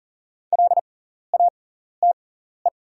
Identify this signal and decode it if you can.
Morse code, heard as a keyed tone, about 40 wpm, spaced as for 14 wpm, 690 Hz. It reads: LATE
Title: 6x Character Spacing - Top 300 Words - 40wpm